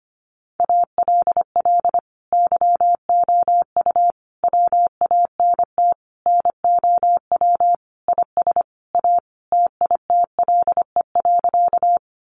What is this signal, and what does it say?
Morse code, heard as a keyed tone, about 25 wpm, 700 Hz. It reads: ALL YOU WANT NOW IS A TITLE.
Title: Sentences from Top 1200 Words 25wpm